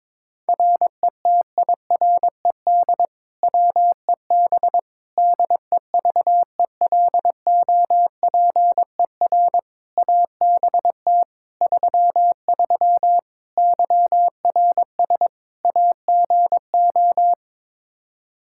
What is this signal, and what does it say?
Morse code, heard as a keyed tone, about 22 wpm, 700 Hz. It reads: RETIRED WEB DEVELOPER ABT 33 YRS AGO
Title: K1USN SST QSO Practice 22wpm